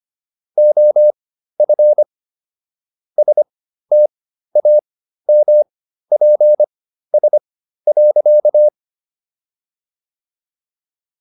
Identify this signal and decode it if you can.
Morse code, heard as a keyed tone, about 25 wpm, spaced as for 13 wpm, 600 Hz. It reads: OF STAMPS.